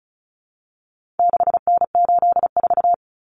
Morse code, heard as a keyed tone, about 34 wpm, 700 Hz. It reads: 6N84
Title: Common Cut Numbers - 4-Digit 35wpm